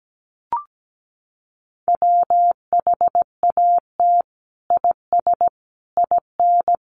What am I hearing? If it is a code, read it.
Morse code, heard as a keyed tone, about 17 wpm, 700 Hz. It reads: WHAT IS IN
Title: Sentences from Top 200 Words 17wpm